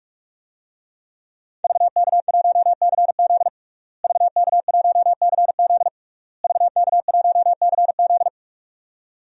Morse code, heard as a keyed tone, about 46 wpm, 700 Hz. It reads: VK1CZ VK1CZ VK1CZ
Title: DXCC-100 Call Signs - 5 Characters - Repeated 3x - 45wpm